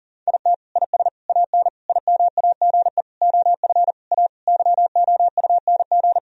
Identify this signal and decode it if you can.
Morse code, heard as a keyed tone, about 40 wpm, 700 Hz. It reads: IT IS AN IMAGE OF A YOUNG
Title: Sentences from Top 1000 Words - Review Part 4 - 40wpm